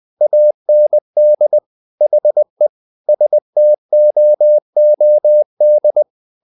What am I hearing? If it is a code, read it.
Morse code, heard as a keyed tone, about 20 wpm, 600 Hz. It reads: AND HE STOOD
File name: the old man and the sea / the old man and the sea part 1 20wpm